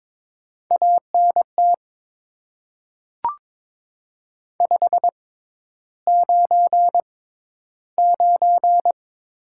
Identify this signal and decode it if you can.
Morse code, heard as a keyed tone, about 22 wpm, spaced as for 11 wpm, 700 Hz. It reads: ANT 599
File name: Top 90 QSO Elements - 6x Character Spacing - 22wpm